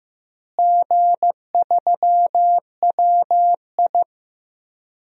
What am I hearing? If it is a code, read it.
Morse code, heard as a keyed tone, about 16 wpm, 700 Hz. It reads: G3WI
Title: DXCC-100 Call Signs - 4 Characters - Repeated 2x - 15wpm